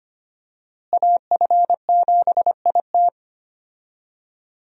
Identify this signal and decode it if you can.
Morse code, heard as a keyed tone, about 25 wpm, 700 Hz. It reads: AF7IT
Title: USA Call Signs - 5 Characters - Repeated 3x - 25wpm